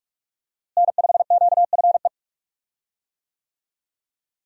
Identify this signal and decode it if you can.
Morse code, heard as a keyed tone, about 46 wpm, 700 Hz. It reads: N5QFE
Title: USA Call Signs - 5 Characters - Repeated 3x - 45wpm